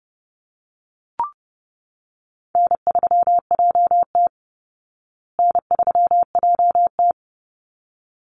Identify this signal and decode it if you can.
Morse code, heard as a keyed tone, about 30 wpm, 700 Hz. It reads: N3JT N3JT
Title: Easy Call Signs Repeated 2x - 30wpm